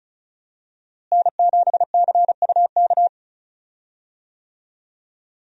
Morse code, heard as a keyed tone, about 34 wpm, 700 Hz. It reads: N7CUK
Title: USA Call Signs - 5 Characters - Repeated 2x - 35wpm